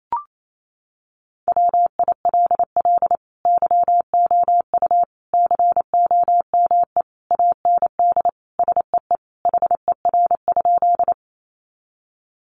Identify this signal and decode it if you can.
Morse code, heard as a keyed tone, about 28 wpm, 700 Hz. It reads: WILL YOU COME AND SEE HER?
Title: Sentences from Top 100 Words 28wpm